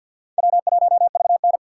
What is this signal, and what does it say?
Morse code, heard as a keyed tone, about 48 wpm, 700 Hz. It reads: W1VN
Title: USA Call Signs - 4 Characters - 2x Character Spacing - 50wpm